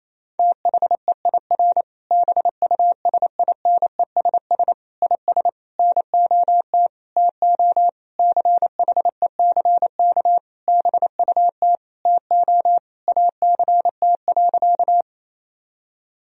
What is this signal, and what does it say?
Morse code, heard as a keyed tone, about 28 wpm, 700 Hz. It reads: THEIR BUSINESS IS NOT TO CHECK BUT TO ACT.
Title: Sentences from Top 1000 Words - Review Part 3 - 28wpm